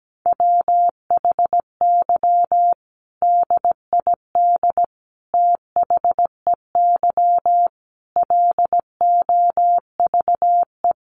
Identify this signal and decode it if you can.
Morse code, heard as a keyed tone, about 17 wpm, 700 Hz. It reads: WHY DID THEY LOVE
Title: Sentences from Top 1500 Words 17wpm